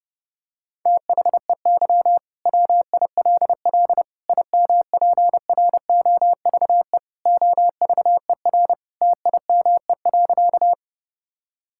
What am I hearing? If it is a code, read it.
Morse code, heard as a keyed tone, about 30 wpm, 700 Hz. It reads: THEY WILL IMPROVE OVER TIME.
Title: Sentences from Top 1000 Words - Review Part 5 - 30wpm